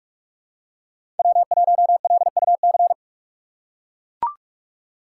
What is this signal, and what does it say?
Morse code, heard as a keyed tone, about 43 wpm, 700 Hz. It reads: W1LUC
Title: USA Call Signs - 5 Characters - 4x Character Spacing - 45wpm